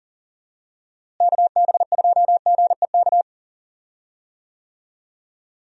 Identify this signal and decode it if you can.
Morse code, heard as a keyed tone, about 40 wpm, 700 Hz. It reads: KB2GEK
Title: USA Call Signs - 6 Characters - Repeated 2x - 40wpm